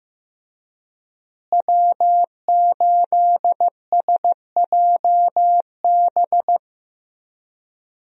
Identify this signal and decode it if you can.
Morse code, heard as a keyed tone, about 15 wpm, 700 Hz. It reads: W8SJB
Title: USA Call Signs - 5 Characters - Repeated 3x - 15wpm